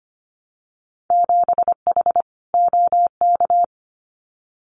Morse code, heard as a keyed tone, about 25 wpm, 700 Hz. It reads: 7H OK
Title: Winter Field Day Exchanges Encore 25wpm